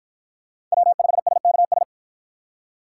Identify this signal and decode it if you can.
Morse code, heard as a keyed tone, about 54 wpm, 700 Hz. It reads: W5SKS